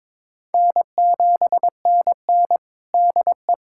Morse code, heard as a keyed tone, about 22 wpm, 700 Hz. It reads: N7NN DE
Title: USA Callsign DE Callsign - 4 Characters - 22wpm